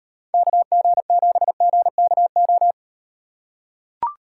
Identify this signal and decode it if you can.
Morse code, heard as a keyed tone, about 37 wpm, 700 Hz. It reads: KG7GKO